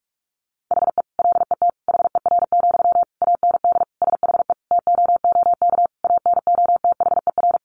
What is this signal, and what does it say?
Morse code, heard as a keyed tone, about 44 wpm, 700 Hz. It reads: HE LET HER, AND SHE TOOK ANOTHER
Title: Sentences from Top 1500 Words 45wpm